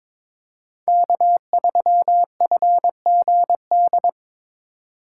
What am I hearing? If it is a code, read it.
Morse code, heard as a keyed tone, about 22 wpm, 700 Hz. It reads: K3FGD
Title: USA Call Signs - 5 Characters - 3x Character Spacing - 22wpm